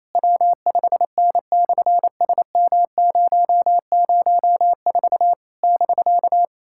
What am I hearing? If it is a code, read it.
Morse code, heard as a keyed tone, about 28 wpm, 700 Hz. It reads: W5N/SM004 <BK>
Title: SOTA QSO Practice 28wpm